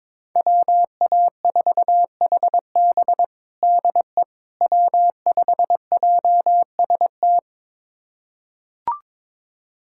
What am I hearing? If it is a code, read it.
Morse code, heard as a keyed tone, about 22 wpm, 700 Hz. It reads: WA4HB DE W5JST
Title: USA Callsign DE Callsign - 5 Characters - 22wpm